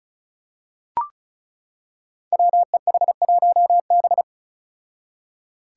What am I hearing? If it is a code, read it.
Morse code, heard as a keyed tone, about 36 wpm, 700 Hz. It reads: WEH1B
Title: Five Challenging Letter-Number Combinations 35wpm